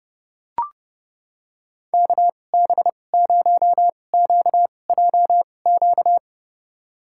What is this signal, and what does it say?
Morse code, heard as a keyed tone, about 30 wpm, spaced as for 22 wpm, 700 Hz. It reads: KB0QJQ
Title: USA Call Signs - 6 Characters - 2x Character Spacing - 30wpm